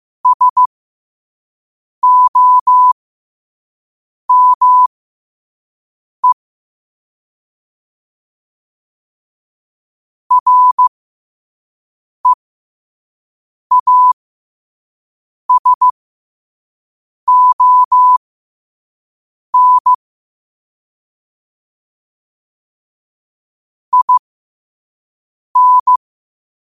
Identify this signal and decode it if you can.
Morse code, heard as a keyed tone, about 15 wpm, spaced as for 5 wpm, 1000 Hz. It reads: SOME REASON IN
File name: qotdfiles / 5wpm